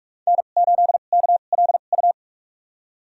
Morse code, heard as a keyed tone, about 41 wpm, spaced as for 31 wpm, 700 Hz. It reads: N8KLU
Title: USA Call Signs - 5 Characters - 2x Character Spacing - 45wpm